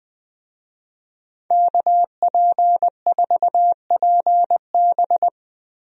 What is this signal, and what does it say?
Morse code, heard as a keyed tone, about 20 wpm, 700 Hz. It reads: KP4PB
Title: DXCC-100 Call Signs - 5 Characters - 20wpm